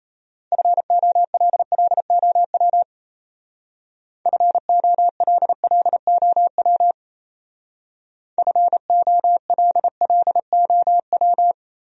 Morse code, heard as a keyed tone, about 34 wpm, 700 Hz. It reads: FOLLOW FOLLOW FOLLOW
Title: Speed-Racing Top 500 Words 28wpm